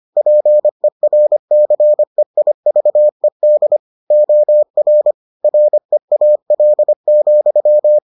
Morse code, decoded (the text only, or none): PERCEIVED OR REAL,